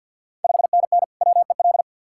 Morse code, heard as a keyed tone, about 51 wpm, 690 Hz. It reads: 5NN WEL
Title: Ontario QSO Party Exchange 50wpm